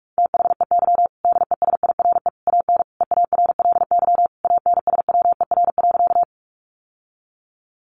Morse code, heard as a keyed tone, about 44 wpm, 700 Hz. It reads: THEY DESIRE AN EARLY ANSWER.
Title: Sentences from Top 1100 Words 45wpm